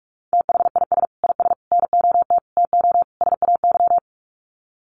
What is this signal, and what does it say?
Morse code, heard as a keyed tone, about 44 wpm, 700 Hz. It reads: THIS IS NOT TO SAY
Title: Sets of 5 Words Encore 45wpm